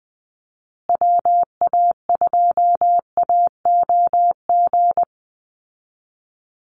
Morse code, heard as a keyed tone, about 19 wpm, 700 Hz. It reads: WA2AOG
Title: Call Signs Repeated 2x - 20wpm